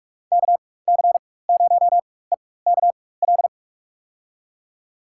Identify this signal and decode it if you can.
Morse code, heard as a keyed tone, about 46 wpm, spaced as for 21 wpm, 700 Hz. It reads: KC0EKL
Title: USA Call Signs - 6 Characters - 4x Character Spacing - 45wpm